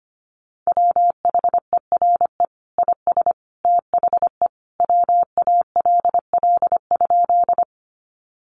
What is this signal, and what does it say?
Morse code, heard as a keyed tone, about 25 wpm, 700 Hz. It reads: WHERE IS THE WALL?
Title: Sentences from Top 1100 Words 25wpm